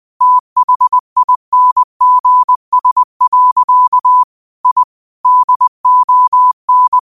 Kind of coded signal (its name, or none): Morse code